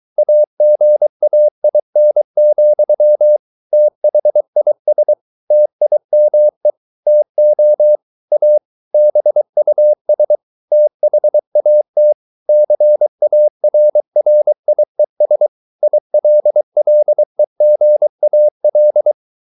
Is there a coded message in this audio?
Yes: Morse code from 0.2 to 19.1 s, heard as a keyed tone, about 23 wpm, 600 Hz. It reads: AGAIN, THIS TIME TO A BUS THAT CARRIES ILLEGAL